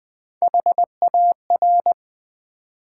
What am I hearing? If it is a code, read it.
Morse code, heard as a keyed tone, about 20 wpm, 700 Hz. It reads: HAR
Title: Build A Word - 1 Letter At A Time - 4-Letter Words - 20wpm